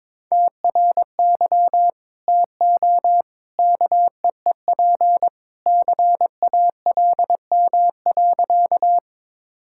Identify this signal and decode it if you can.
Morse code, heard as a keyed tone, about 22 wpm, 700 Hz. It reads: TRY TO KEEP CALM.
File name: practice / 22wpm / Sentences with 2-Word Phrases - 22wpm